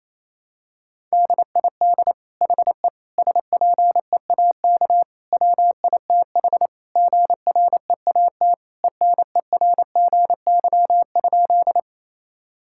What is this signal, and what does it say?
Morse code, heard as a keyed tone, about 28 wpm, 700 Hz. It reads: DID HE SPEAK WITH GREAT ENERGY?
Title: Sentences from Top 1000 Words - Review Part 6 - 28wpm